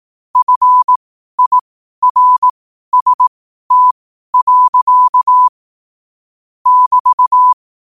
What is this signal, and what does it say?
Morse code, heard as a keyed tone, about 18 wpm, spaced as for 12 wpm, 1000 Hz. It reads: FIRST. <BT>
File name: qotdfiles / 12wpm